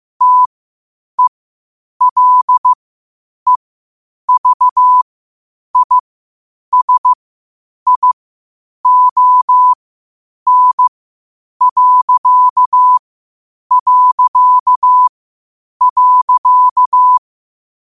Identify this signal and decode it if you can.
Morse code, heard as a keyed tone, about 15 wpm, spaced as for 8 wpm, 1000 Hz. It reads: TELEVISION...
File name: qotdfiles / 8wpm